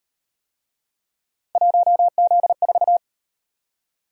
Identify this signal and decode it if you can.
Morse code, heard as a keyed tone, about 38 wpm, 700 Hz. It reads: 1Z4